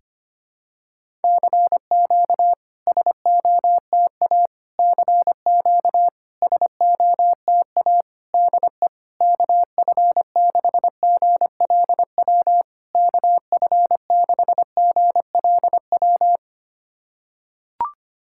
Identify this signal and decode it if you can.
Morse code, heard as a keyed tone, about 25 wpm, 690 Hz. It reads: CQ SOTA CQ SOTA DE KF6GLW KF6GLW
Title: SOTA QSO Practice 25wpm